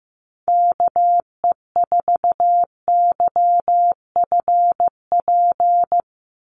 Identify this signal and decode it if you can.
Morse code, heard as a keyed tone, about 15 wpm, 700 Hz. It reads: KE4YFP
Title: USA Call Signs 15wpm